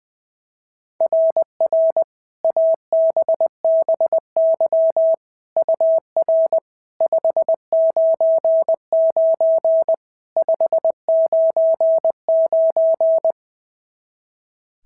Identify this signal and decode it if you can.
Morse code, heard as a keyed tone, about 21 wpm, 660 Hz. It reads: RR ABBY UR 599 599
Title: SOTA QSO Practice 20wpm